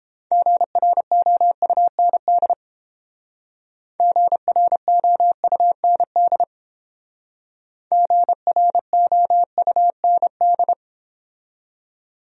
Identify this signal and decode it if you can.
Morse code, heard as a keyed tone, about 32 wpm, 700 Hz. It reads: GROUND GROUND GROUND